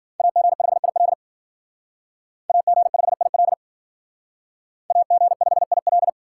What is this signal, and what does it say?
Morse code, heard as a keyed tone, about 53 wpm, 700 Hz. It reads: AG5IL AG5IL AG5IL